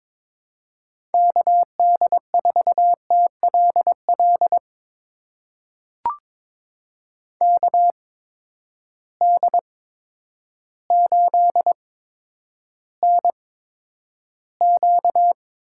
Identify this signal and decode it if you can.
Morse code, heard as a keyed tone, about 22 wpm, spaced as for 10 wpm, 700 Hz. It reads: KD4TLL KD8NQ
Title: USA Call Signs - 6 Characters - 8x Character Spacing - 22wpm